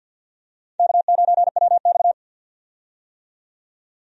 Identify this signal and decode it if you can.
Morse code, heard as a keyed tone, about 46 wpm, 700 Hz. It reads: K9WX
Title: USA Call Signs - 4 Characters - 50wpm